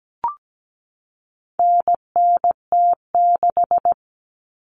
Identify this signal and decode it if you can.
Morse code, heard as a keyed tone, about 17 wpm, 700 Hz. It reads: NNT6
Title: Common Cut Numbers - 4-Digit 17wpm